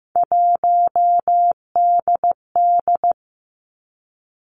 Morse code, heard as a keyed tone, about 15 wpm, 700 Hz. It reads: 1DD
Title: Three Challenging Letter-Number Combinations 15wpm